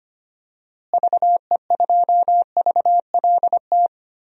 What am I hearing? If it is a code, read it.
Morse code, heard as a keyed tone, about 25 wpm, 700 Hz. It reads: VE2VLT
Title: DXCC-100 Call Signs - 6 Characters - Repeated 2x - 25wpm